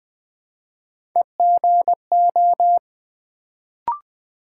Morse code, heard as a keyed tone, about 20 wpm, 700 Hz. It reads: EGO
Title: Build A Word - 1 Letter At A Time - 3-Letter Words - 20wpm